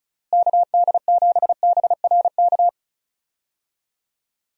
Morse code, heard as a keyed tone, about 36 wpm, 700 Hz. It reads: KD7BRK
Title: USA Call Signs - 6 Characters - Repeated 3x - 35wpm